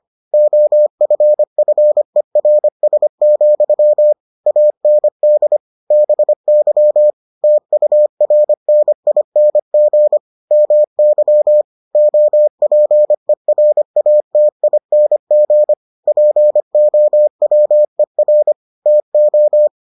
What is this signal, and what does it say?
Morse code, heard as a keyed tone, about 25 wpm, 600 Hz. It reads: OFFERS, AND BY TURNING MY OPERATING POWER TO